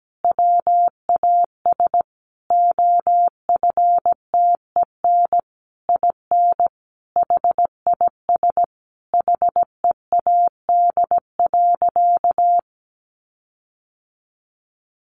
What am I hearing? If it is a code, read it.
Morse code, heard as a keyed tone, about 17 wpm, 700 Hz. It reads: WAS OFTEN IN HIS HEAD.
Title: Sentences from Top 1400 Words 17wpm